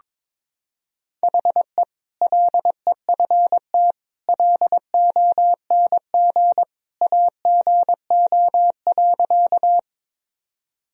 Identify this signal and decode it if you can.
Morse code, heard as a keyed tone, about 22 wpm, 700 Hz. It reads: HE LEFT LONG AGO.
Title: Sentences from Top 1000 Words - Review Part 3 - 22wpm